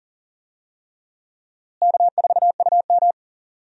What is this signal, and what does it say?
Morse code, heard as a keyed tone, about 38 wpm, 700 Hz. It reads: K4UM
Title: USA Call Signs - 4 Characters - Repeated 3x - 40wpm